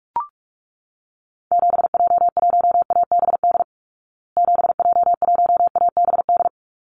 Morse code, heard as a keyed tone, about 44 wpm, 700 Hz. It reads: 7J1ABD 7J1ABD
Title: Call Signs Repeated 3x - 45wpm